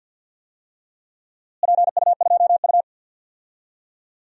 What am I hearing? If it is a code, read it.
Morse code, heard as a keyed tone, about 46 wpm, 700 Hz. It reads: PU2U
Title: DXCC-100 Call Signs - 4 Characters - Repeated 3x - 50wpm